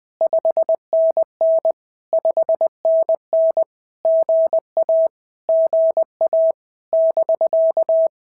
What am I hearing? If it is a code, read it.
Morse code, heard as a keyed tone, about 19 wpm, 650 Hz. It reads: 5NN 5NN GA GA <BK>
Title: POTA QSO Practice - 20wpm